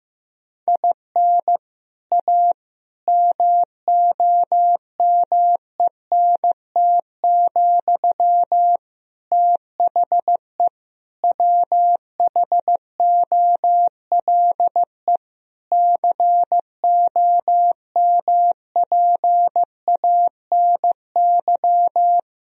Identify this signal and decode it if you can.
Morse code, heard as a keyed tone, about 15 wpm, 700 Hz. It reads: IN A MOMENT, THE WHOLE COMPANY
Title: Sentences from Top 1000 Words - Review Part 3 - 15wpm